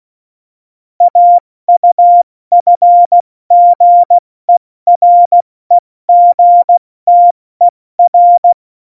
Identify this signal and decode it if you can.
Morse code, heard as a keyed tone, about 16 wpm, 700 Hz. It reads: AUFGEREGTER
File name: Morsekurs / Witze / Witz 28. Januar 2020, 70 F80